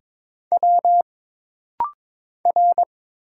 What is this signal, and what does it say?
Morse code, heard as a keyed tone, about 22 wpm, 700 Hz. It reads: W R
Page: This rapid-fire timing is geared to push your ICR (Instant Character Recognition) skills.